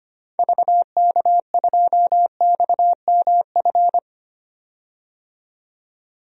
Morse code, heard as a keyed tone, about 25 wpm, 690 Hz. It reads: VK2XMF
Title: DXCC-100 Call Signs - 6 Characters - 25wpm